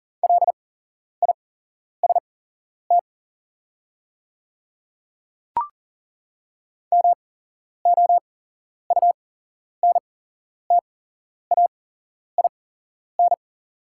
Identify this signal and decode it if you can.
Morse code, heard as a keyed tone, about 40 wpm, spaced as for 11 wpm, 700 Hz. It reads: LIST MOUNTAIN
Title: 8x Character Spacing - Top 500 Words - 40wpm